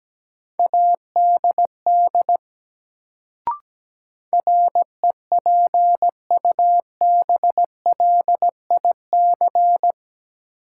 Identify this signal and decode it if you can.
Morse code, heard as a keyed tone, about 17 wpm, 700 Hz. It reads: ADD REPUBLIC
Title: Sentences from Top 1500 Words - New Word Drill Alt - 17wpm